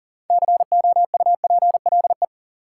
Morse code, heard as a keyed tone, about 40 wpm, 700 Hz. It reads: COUPLE
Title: Sentences from Top 1100 Words - New Word Drill - 40wpm